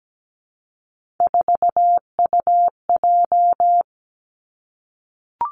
Morse code, heard as a keyed tone, about 17 wpm, 700 Hz. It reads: 4UJ
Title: Three Challenging Letter-Number Combinations 17wpm